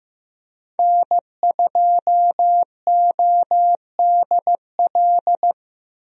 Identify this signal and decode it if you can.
Morse code, heard as a keyed tone, about 15 wpm, 700 Hz. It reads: N2ODL
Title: USA Call Signs - 5 Characters - Repeated 2x - 15wpm